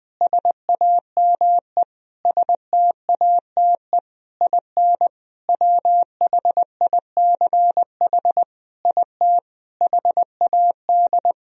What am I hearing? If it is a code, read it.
Morse code, heard as a keyed tone, about 20 wpm, 700 Hz. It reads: SAME STATE IN WHICH IT HAD
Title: Sentences from Top 1000 Words - Review Part 2 - 20wpm